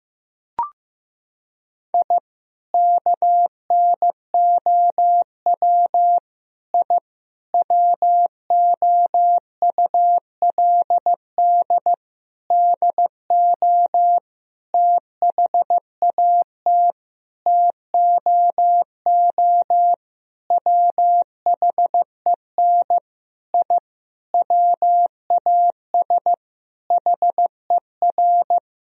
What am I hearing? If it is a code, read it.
Morse code, heard as a keyed tone, about 15 wpm, 700 Hz. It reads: I KNOW I WOULD DO THAT TOO WHEN I WAS HER
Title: Sentences from Top 1000 Words - Review Part 3 - 15wpm